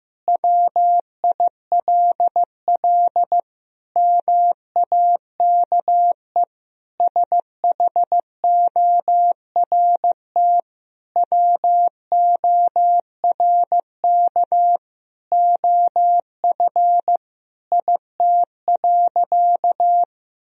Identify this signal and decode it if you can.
Morse code, heard as a keyed tone, about 15 wpm, 700 Hz. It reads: WILL MAKE SHORT WORK OF IT.
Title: Sentences from Top 1000 Words - Review Part 2 - 15wpm